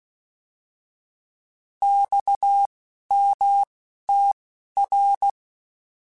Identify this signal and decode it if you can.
Morse code, heard as a keyed tone, about 16 wpm, spaced as for 11 wpm, 800 Hz. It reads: XMTR
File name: Abreviations / Abreviations mp3s / Abreviations@10